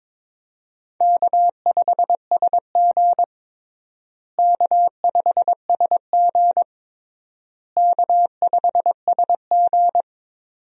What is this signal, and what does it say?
Morse code, heard as a keyed tone, about 22 wpm, 700 Hz. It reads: K5SG K5SG K5SG
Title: USA Call Signs - 4 Characters - Repeated 3x - 22wpm